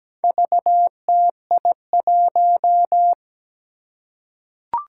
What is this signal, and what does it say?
Morse code, heard as a keyed tone, about 17 wpm, 700 Hz. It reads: VTI1
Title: Four Challenging Letter-Number Combinations 17wpm